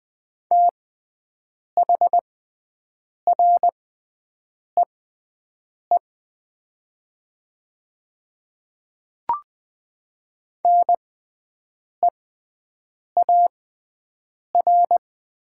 Morse code, heard as a keyed tone, about 21 wpm, spaced as for 7 wpm, 700 Hz. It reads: THREE NEAR
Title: 6x Character Spacing - Top 500 Words - 20wpm